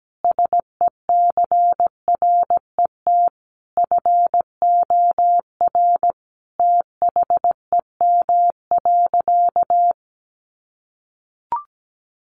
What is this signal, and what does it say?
Morse code, heard as a keyed tone, about 17 wpm, 700 Hz. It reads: SECRET FOR THEM.
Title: Sentences from Top 1200 Words 17wpm